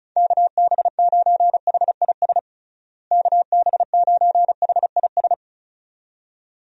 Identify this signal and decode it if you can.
Morse code, heard as a keyed tone, about 36 wpm, 700 Hz. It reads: KB9HIS KB9HIS
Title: USA Call Signs - 6 Characters - Repeated 2x - 35wpm